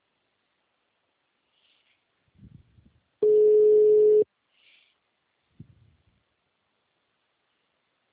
Ringback tone (Europe, 425 Hz).